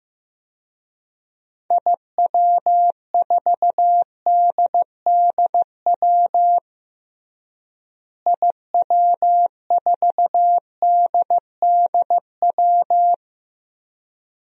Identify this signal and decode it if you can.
Morse code, heard as a keyed tone, about 15 wpm, 700 Hz. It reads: IW4DDW IW4DDW
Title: DXCC-100 Call Signs - 6 Characters - Repeated 2x - 15wpm